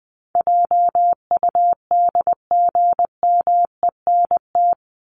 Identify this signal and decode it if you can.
Morse code, heard as a keyed tone, about 20 wpm, 700 Hz. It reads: JUDGMENT
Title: Top 1751-2000 Words 20wpm